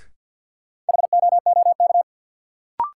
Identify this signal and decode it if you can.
Morse code, heard as a keyed tone, about 49 wpm, 700 Hz. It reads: HOOK
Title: Sentences from Top 1600 Words - New Word Drill - 50wpm